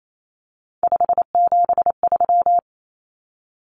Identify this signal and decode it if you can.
Morse code, heard as a keyed tone, about 28 wpm, 700 Hz. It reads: 573
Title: Numbers - 3-Digit 28wpm